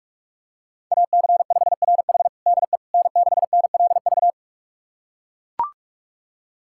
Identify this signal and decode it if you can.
Morse code, heard as a keyed tone, about 42 wpm, 700 Hz. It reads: AC5RH DE N6NLV
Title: USA Callsign DE Callsign 45wpm